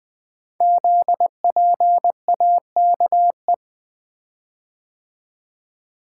Morse code, heard as a keyed tone, about 20 wpm, 700 Hz. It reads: ZPAKE
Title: Code Groups - 5 Letter:Numbers - 20wpm